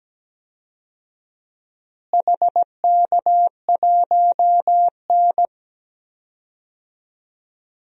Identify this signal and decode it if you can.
Morse code, heard as a keyed tone, about 17 wpm, 700 Hz. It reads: HK1N
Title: DXCC-100 Call Signs - 4 Characters - Repeated 2x - 17wpm